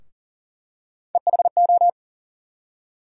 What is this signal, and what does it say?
Morse code, heard as a keyed tone, about 40 wpm, 700 Hz. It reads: EHO
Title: USA Call Sign Postfixes - 3 Letters 40wpm